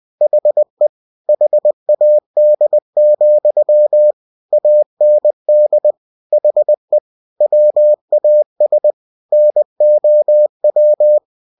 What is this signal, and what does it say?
Morse code, heard as a keyed tone, about 20 wpm, 600 Hz. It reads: HE HAD, AND HE WAS NOW